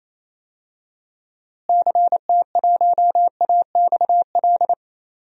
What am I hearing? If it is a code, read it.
Morse code, heard as a keyed tone, about 28 wpm, 700 Hz. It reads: CT1AXL
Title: DXCC-100 Call Signs - 6 Characters - Repeated 3x - 28wpm